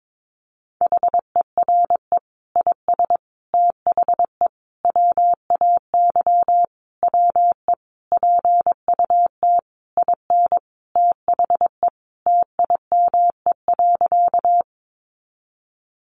Morse code, heard as a keyed tone, about 22 wpm, 700 Hz. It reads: HERE IS THE WAY WE PUT IN THE TIME.
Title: Sentences from Top 200 Words 22wpm